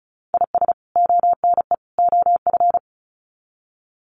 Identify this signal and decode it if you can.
Morse code, heard as a keyed tone, about 34 wpm, 700 Hz. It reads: IS ONE OF
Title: Sets of 3 Words 35wpm